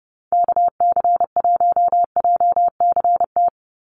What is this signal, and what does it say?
Morse code, heard as a keyed tone, about 30 wpm, 700 Hz. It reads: KC1JCT
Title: USA Call Signs 30wpm